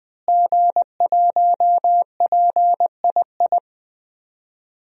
Morse code, heard as a keyed tone, about 20 wpm, 700 Hz. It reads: G1PII
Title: DXCC-100 Call Signs - 5 Characters - Repeated 3x - 20wpm